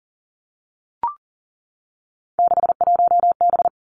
Morse code, heard as a keyed tone, about 41 wpm, 700 Hz. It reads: 61B
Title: Three Challenging Letter-Number Combinations 40wpm